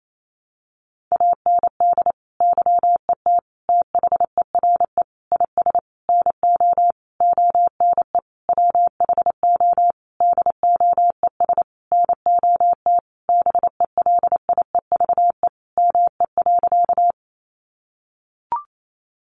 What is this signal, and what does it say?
Morse code, heard as a keyed tone, about 28 wpm, 700 Hz. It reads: AND YET THERE IS NO ONE WHO DOES NOT BELIEVE ME.
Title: Sentences from Top 1000 Words 28wpm